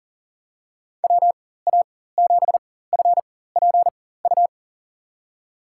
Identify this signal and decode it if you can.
Morse code, heard as a keyed tone, about 40 wpm, spaced as for 19 wpm, 700 Hz. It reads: WA7FPU
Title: USA Call Signs - 6 Characters - 4x Character Spacing - 40wpm